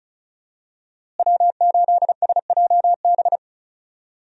Morse code, heard as a keyed tone, about 36 wpm, 700 Hz. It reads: W8SJB
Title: USA Call Signs - 5 Characters - Repeated 3x - 35wpm